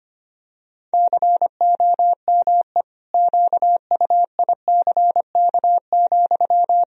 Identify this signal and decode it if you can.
Morse code, heard as a keyed tone, about 25 wpm, 700 Hz. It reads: COME QUICK,
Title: Sentences from Top 1000 Words - Review Part 3 - 25wpm